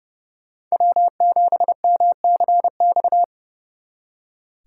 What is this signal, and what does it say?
Morse code, heard as a keyed tone, about 37 wpm, 700 Hz. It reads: W7MCX
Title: Speed Racing - USA 1x3 Call Signs 30wpm